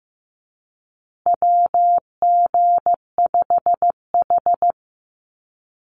Morse code, heard as a keyed tone, about 15 wpm, 700 Hz. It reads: WG5H
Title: Call Signs Repeated 3x - 15wpm